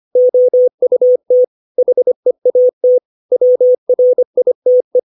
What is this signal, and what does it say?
Morse code, heard as a keyed tone, about 25 wpm, 500 Hz. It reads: OUT HEAT WRITE
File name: 632 words / 632 Words 25WPM